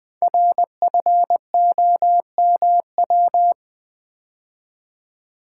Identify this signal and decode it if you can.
Morse code, heard as a keyed tone, about 21 wpm, 700 Hz. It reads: RFOMW
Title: Code Groups - 5 Letters - 20wpm